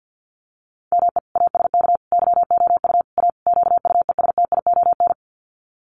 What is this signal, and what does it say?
Morse code, heard as a keyed tone, about 50 wpm, 700 Hz. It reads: ME ASK YOU A QUESTION
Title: Sets of 5 Words Encore2 50wpm